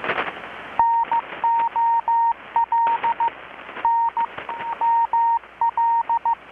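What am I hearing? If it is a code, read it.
Morse code, heard as a keyed tone, about 15 wpm, 930 Hz. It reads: NOL N ML
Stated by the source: Morse from KPH's Night of Nights XX maritime radiotelegraphy event, received in Chiapas, Mexico about 3,800 km away on a Kenwood R-600 with a grounded 10 m long-wire antenna.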